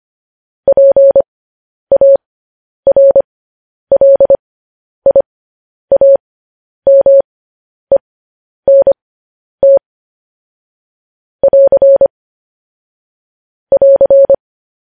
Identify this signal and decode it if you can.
Morse code, heard as a keyed tone, about 25 wpm, spaced as for 10 wpm, 550 Hz. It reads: PARLIAMENT <AR> <AR>
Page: news-headlines-in-morse-code-at-10-wpm